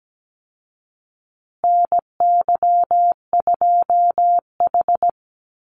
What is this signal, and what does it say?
Morse code, heard as a keyed tone, about 17 wpm, 700 Hz. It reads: NY2H
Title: Call Signs Repeated 2x - 17wpm